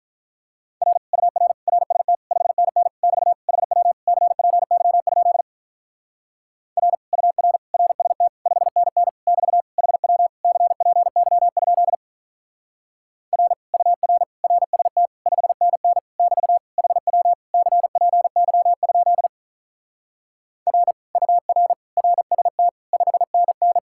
Morse code, heard as a keyed tone, about 46 wpm, 700 Hz. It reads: R UR RST 5NN <BT> HW CPY? R UR RST 5NN <BT> HW CPY? R UR RST 5NN <BT> HW CPY? R UR RST 5NN